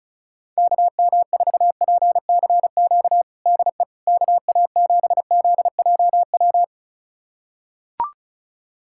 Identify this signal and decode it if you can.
Morse code, heard as a keyed tone, about 36 wpm, 700 Hz. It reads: KM4PCQ DE KA7ZJW
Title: USA Callsign DE Callsign - 6 Characters - 35wpm